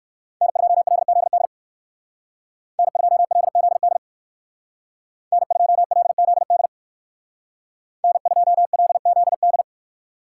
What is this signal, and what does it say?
Morse code, heard as a keyed tone, about 59 wpm, 700 Hz. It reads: N2LZD N2LZD N2LZD N2LZD